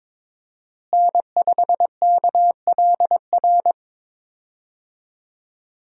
Morse code, heard as a keyed tone, about 22 wpm, 700 Hz. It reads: N5KLR
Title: USA Call Signs - 5 Characters - Repeated 3x - 22wpm